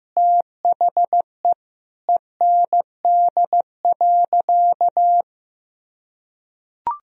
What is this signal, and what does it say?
Morse code, heard as a keyed tone, about 16 wpm, 700 Hz. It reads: THE END.
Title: Sentences from Top 1000 Words - Review Part 2 - 15wpm